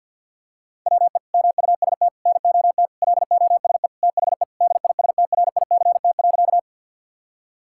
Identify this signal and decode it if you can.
Morse code, heard as a keyed tone, about 50 wpm, 700 Hz. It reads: WE MUST NOT LOSE THE DISTRICT.
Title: Sentences from Top 1600 Words 50wpm